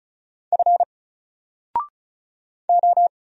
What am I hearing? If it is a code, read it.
Morse code, heard as a keyed tone, about 33 wpm, 700 Hz. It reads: F O
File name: Single-Letter-Number-35wpm